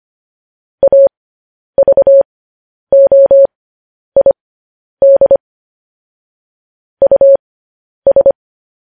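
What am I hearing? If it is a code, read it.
Morse code, heard as a keyed tone, about 25 wpm, spaced as for 10 wpm, 550 Hz. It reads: AVOID US
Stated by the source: Morse code news headlines sent at 10.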